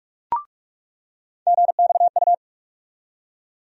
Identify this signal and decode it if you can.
Morse code, heard as a keyed tone, about 46 wpm, 700 Hz. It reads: GXU
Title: USA Call Sign Postfixes - 3 Letters 45wpm